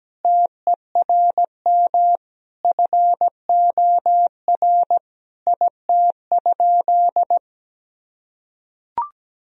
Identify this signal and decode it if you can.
Morse code, heard as a keyed tone, about 17 wpm, 700 Hz. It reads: TERM FOR IT?
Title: Sentences from Top 1000 Words - Review Part 4 - 17wpm